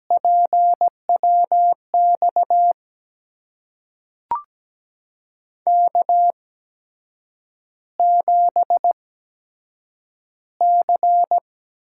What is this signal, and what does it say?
Morse code, heard as a keyed tone, about 17 wpm, spaced as for 7 wpm, 700 Hz. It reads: PWX K7C
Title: USA Call Signs - 8x Character Spacing - 17wpm